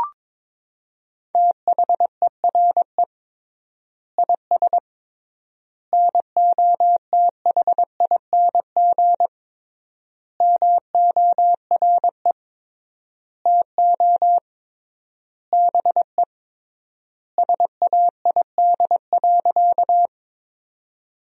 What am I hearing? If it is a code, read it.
Morse code, heard as a keyed tone, about 22 wpm, 700 Hz. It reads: THERE IS NOTHING MORE TO BE SAID.
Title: 3x Word-Spacing - Sentences from Top 500 Words - 22wpm